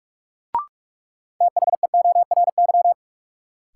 Morse code, heard as a keyed tone, about 44 wpm, 700 Hz. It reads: THEORY
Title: Sentences from Top 600 Words - New Word Drill - 45wpm